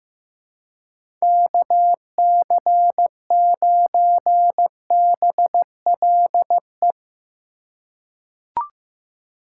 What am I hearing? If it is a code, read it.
Morse code, heard as a keyed tone, about 15 wpm, 700 Hz. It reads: KC9BLE
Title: USA Call Signs - 4x Character Spacing - 15wpm